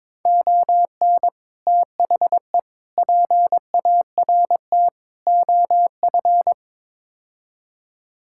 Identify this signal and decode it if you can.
Morse code, heard as a keyed tone, about 22 wpm, 700 Hz. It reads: ON THE PART OF
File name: practice / 22wpm / Build A Phrase - 5-Word Phrases - 22wpm